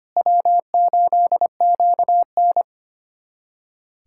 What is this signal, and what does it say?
Morse code, heard as a keyed tone, about 25 wpm, 700 Hz. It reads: W8QN
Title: USA Callsign DE Callsign - 4 Characters - 25wpm